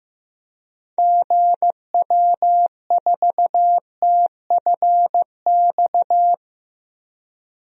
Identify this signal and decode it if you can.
Morse code, heard as a keyed tone, about 16 wpm, 700 Hz. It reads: GW4TFX
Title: DXCC-100 Call Signs - 6 Characters - Repeated 2x - 15wpm